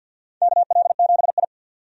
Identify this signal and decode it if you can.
Morse code, heard as a keyed tone, about 51 wpm, 700 Hz. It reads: KL7I
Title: DXCC-100 Call Signs - 4 Characters - 50wpm